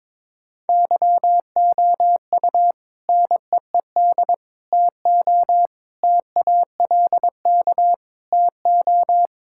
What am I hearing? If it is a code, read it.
Morse code, heard as a keyed tone, about 22 wpm, 700 Hz. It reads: YOU NEED TO TALK TO
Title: Sentences from Top 1600 Words 22wpm